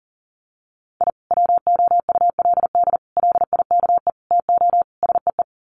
Morse code, heard as a keyed tone, about 40 wpm, 700 Hz. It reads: I WOULD LIKE TO SEE
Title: Sets of 5 Words 40wpm